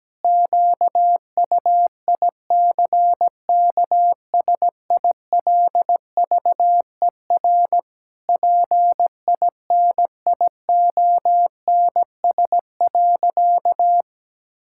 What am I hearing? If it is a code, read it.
Morse code, heard as a keyed tone, about 17 wpm, 700 Hz. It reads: QUICKSILVER PINIONS.
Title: Pangram Challenge v2 - 17wpm